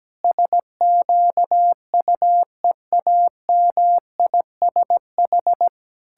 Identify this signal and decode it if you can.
Morse code, heard as a keyed tone, about 17 wpm, 700 Hz. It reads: SQUEAMISH
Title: Pangram Challenge v2 - 17wpm